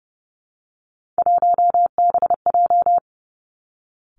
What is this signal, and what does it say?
Morse code, heard as a keyed tone, about 30 wpm, 700 Hz. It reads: 1BJ
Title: Three Challenging Letter-Number Combinations 30wpm